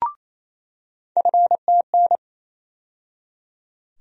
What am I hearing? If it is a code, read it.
Morse code, heard as a keyed tone, about 28 wpm, 700 Hz. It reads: FTN
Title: USA Call Sign Postfixes - 3 Letters 28wpm